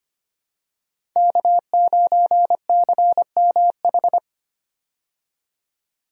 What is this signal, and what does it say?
Morse code, heard as a keyed tone, about 25 wpm, 700 Hz. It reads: K9CMH
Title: USA Call Signs - 5 Characters - 25wpm